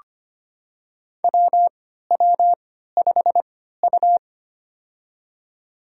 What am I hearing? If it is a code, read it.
Morse code, heard as a keyed tone, about 25 wpm, spaced as for 14 wpm, 700 Hz. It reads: WW5U
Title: USA Call Signs - 4 Characters - 3x Character Spacing - 25wpm